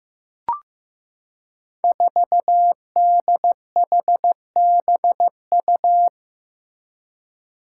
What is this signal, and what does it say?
Morse code, heard as a keyed tone, about 15 wpm, 700 Hz. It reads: 4DHBU
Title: Five Challenging Letter-Number Combinations 15wpm